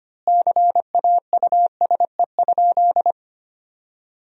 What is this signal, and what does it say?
Morse code, heard as a keyed tone, about 25 wpm, 700 Hz. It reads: CAUSE?
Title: 3x Word-Spacing - Sentences from Top 200 Words - 25wpm